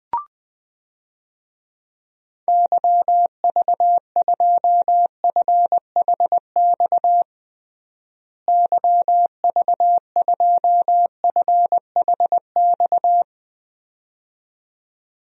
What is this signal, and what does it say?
Morse code, heard as a keyed tone, about 20 wpm, 700 Hz. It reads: YV2FHX YV2FHX
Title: DXCC-100 Call Signs - 6 Characters - Repeated 2x - 20wpm